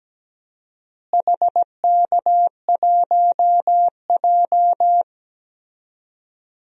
Morse code, heard as a keyed tone, about 17 wpm, 700 Hz. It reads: HK1J
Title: DXCC-100 Call Signs - 4 Characters - Repeated 2x - 17wpm